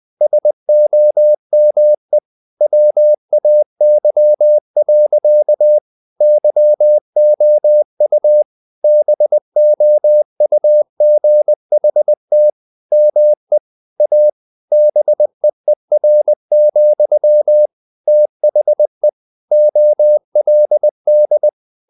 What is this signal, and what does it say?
Morse code, heard as a keyed tone, about 20 wpm, 600 Hz. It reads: SOME WAY. YOU BOUGHT ME A BEER, THE OLD
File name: the old man and the sea / the old man and the sea part 1 20wpm